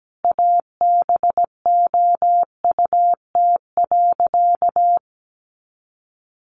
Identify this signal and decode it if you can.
Morse code, heard as a keyed tone, about 17 wpm, 700 Hz. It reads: ABOUT.
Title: Sentences from Top 400 Words 17wpm